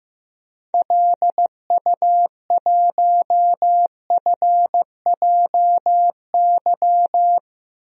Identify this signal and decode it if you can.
Morse code, heard as a keyed tone, about 16 wpm, 700 Hz. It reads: LU1FJY
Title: DXCC-100 Call Signs - 6 Characters - 15wpm